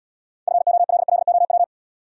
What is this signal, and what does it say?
Morse code, heard as a keyed tone, about 69 wpm, 700 Hz. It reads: FOLLOW